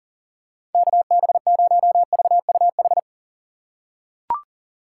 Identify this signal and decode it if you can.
Morse code, heard as a keyed tone, about 37 wpm, 700 Hz. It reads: KB0VUH
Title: USA Call Signs - 6 Characters - 6x Character Spacing - 40wpm